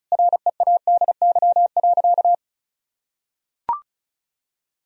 Morse code, heard as a keyed tone, about 36 wpm, 700 Hz. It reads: READY.
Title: Sentences from Top 1000 Words - Review Part 1 - 35wpm